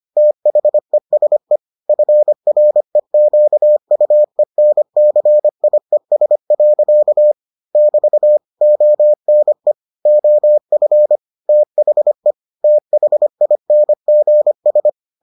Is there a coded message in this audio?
Yes: Morse code from 0.2 to 14.9 s, heard as a keyed tone, about 25 wpm, 600 Hz. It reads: THESE FREQUENCIES. <BT> ONE OF THE THINGS